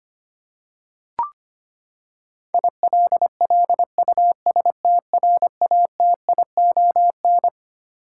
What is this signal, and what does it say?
Morse code, heard as a keyed tone, about 25 wpm, 700 Hz. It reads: ILLUSTRATION
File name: Twelve Letter Words - 25wpm